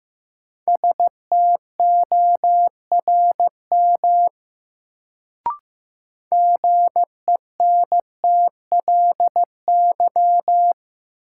Morse code, heard as a keyed tone, about 15 wpm, 700 Hz. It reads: STORM GENTLY
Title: Sentences from Top 1500 Words - New Word Drill Alt - 15wpm